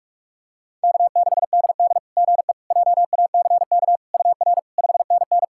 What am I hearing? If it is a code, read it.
Morse code, heard as a keyed tone, about 46 wpm, 690 Hz. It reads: K6DD GE JACK UR 5NN
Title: SOTA QSO Practice 45wpm